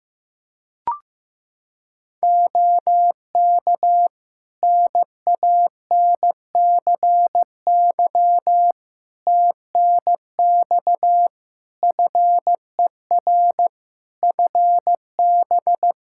Morse code, heard as a keyed tone, about 15 wpm, 700 Hz. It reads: OK NANCY TNX FER FB
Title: Practice QSO - Exchange 4 - Ending - 15wpm